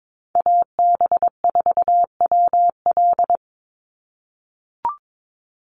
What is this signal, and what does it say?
Morse code, heard as a keyed tone, about 22 wpm, 700 Hz. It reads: AB4WL
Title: NA SOTA Chaser Callsigns 22wpm